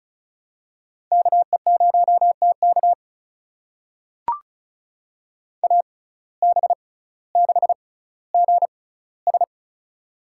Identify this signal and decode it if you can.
Morse code, heard as a keyed tone, about 36 wpm, spaced as for 18 wpm, 700 Hz. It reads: KE0TK AB6GS
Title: USA Call Signs - 5 Characters - 6x Character Spacing - 35wpm